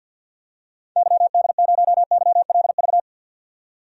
Morse code, heard as a keyed tone, about 50 wpm, 700 Hz. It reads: YD0YLV
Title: DXCC-100 Call Signs - 6 Characters - 50wpm